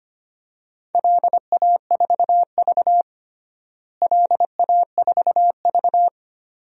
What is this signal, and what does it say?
Morse code, heard as a keyed tone, about 25 wpm, 700 Hz. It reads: LA4V LA4V
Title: DXCC-100 Call Signs - 4 Characters - Repeated 2x - 25wpm